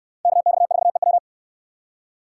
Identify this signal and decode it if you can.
Morse code, heard as a keyed tone, about 64 wpm, 700 Hz. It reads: KX4EW